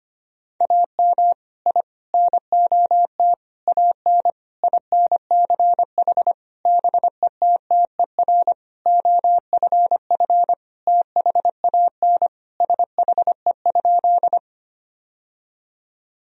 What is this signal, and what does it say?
Morse code, heard as a keyed tone, about 25 wpm, 700 Hz. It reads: AM I NOT AN INCH BETTER OFF THAN SHE?
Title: Sentences from Top 1000 Words - Review Part 3 - 25wpm